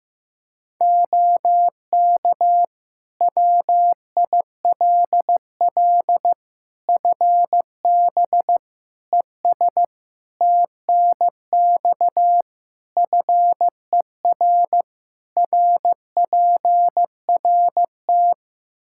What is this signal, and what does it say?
Morse code, heard as a keyed tone, about 15 wpm, 700 Hz. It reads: OK WILL FB ES TNX FER RPRT
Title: Practice QSO - Exchange 2 - Info - 15wpm